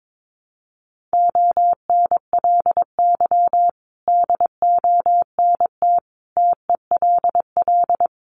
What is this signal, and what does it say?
Morse code, heard as a keyed tone, about 22 wpm, 700 Hz. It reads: ONLY DONT TELL
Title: Sentences from Top 900 Words 22wpm